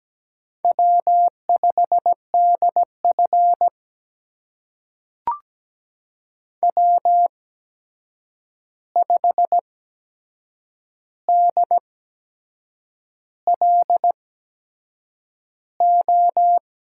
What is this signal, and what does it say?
Morse code, heard as a keyed tone, about 17 wpm, spaced as for 7 wpm, 700 Hz. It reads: W5DF W5DLO
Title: USA Call Signs - 8x Character Spacing - 17wpm